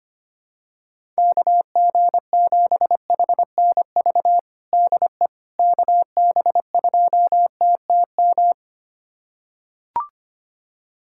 Morse code, heard as a keyed tone, about 25 wpm, 700 Hz. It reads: KG7HNV DE KB2TTM
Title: USA Callsign DE Callsign - 6 Characters - 25wpm